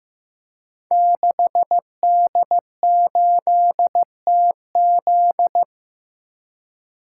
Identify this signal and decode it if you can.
Morse code, heard as a keyed tone, about 15 wpm, 700 Hz. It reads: 6D8TZ
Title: Code Groups - 5 Letter:Numbers - 15wpm